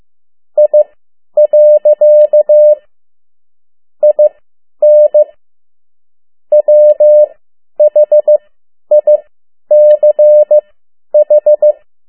Morse code, heard as a keyed tone, about 15 wpm, spaced as for 10 wpm, 600 Hz. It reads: I. IN WHICH
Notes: Around the World in 80 Days paragraph one Morse code 10 WPM